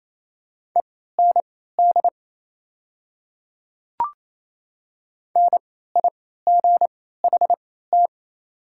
Morse code, heard as a keyed tone, about 27 wpm, spaced as for 16 wpm, 690 Hz. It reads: END NIGHT
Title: 3x Character Spacing - Top 300 Words - 28wpm